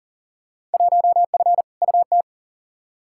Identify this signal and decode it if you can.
Morse code, heard as a keyed tone, about 37 wpm, 700 Hz. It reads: 1F UT
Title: Field Day Exchanges Encore - 40wpm